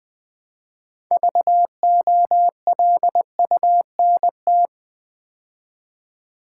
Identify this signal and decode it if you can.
Morse code, heard as a keyed tone, about 20 wpm, 700 Hz. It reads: VOLUNT
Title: Build A Word - 3 Letters At A Time - 9-Letter Words - 20wpm